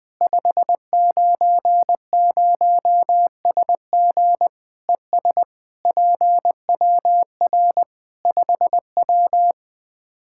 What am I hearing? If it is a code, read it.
Morse code, heard as a keyed tone, about 21 wpm, 700 Hz. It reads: 590SG ES PWR 5W
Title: Practice QSO - Exchange 2 - Info - 20wpm